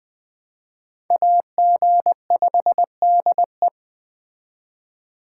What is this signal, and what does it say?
Morse code, heard as a keyed tone, about 21 wpm, 700 Hz. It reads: AG5DE
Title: Speed Racing - USA 2x2 Call Signs 17wpm